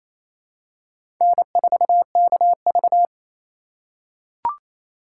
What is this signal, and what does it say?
Morse code, heard as a keyed tone, about 28 wpm, 700 Hz. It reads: N4KV
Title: USA Call Signs - 4 Characters - 3x Character Spacing - 28wpm